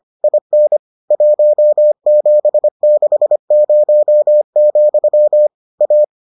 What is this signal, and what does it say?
Morse code, heard as a keyed tone, about 25 wpm, 600 Hz. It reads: IN 1760, A